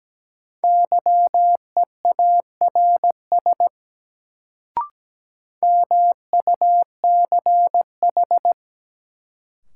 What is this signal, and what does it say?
Morse code, heard as a keyed tone, about 17 wpm, 700 Hz. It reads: YEARS MUCH
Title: Sentences from Top 200 Words - New Word Drill - 17wpm